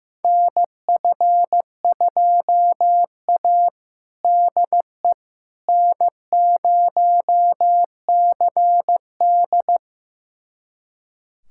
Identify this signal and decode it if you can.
Morse code, heard as a keyed tone, about 15 wpm, 700 Hz. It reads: NF2A DE N0CD
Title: USA Callsign DE Callsign - 4 Characters - 15wpm